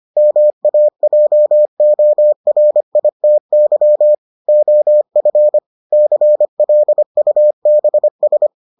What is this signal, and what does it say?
Morse code, heard as a keyed tone, about 25 wpm, 600 Hz. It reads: MAJORITY OF CLUBS